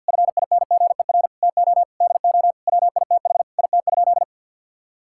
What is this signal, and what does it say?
Morse code, heard as a keyed tone, about 50 wpm, 700 Hz. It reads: FINGER TO DO WITH IT?
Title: Sentences from Top 1600 Words 50wpm